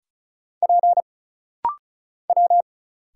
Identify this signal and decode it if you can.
Morse code, heard as a keyed tone, about 36 wpm, 700 Hz. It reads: P W
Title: Single Letter-Number Alt - Rapid-Fire 35wpm